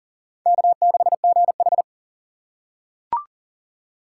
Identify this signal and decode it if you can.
Morse code, heard as a keyed tone, about 40 wpm, 700 Hz. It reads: K6GH
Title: USA Call Signs - 4 Characters - 3x Character Spacing - 40wpm